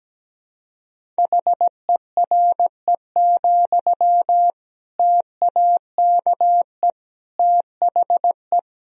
Morse code, heard as a keyed tone, about 17 wpm, 700 Hz. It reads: HERE, TAKE THE
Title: Sentences from Top 1000 Words - Review Part 1 - 17wpm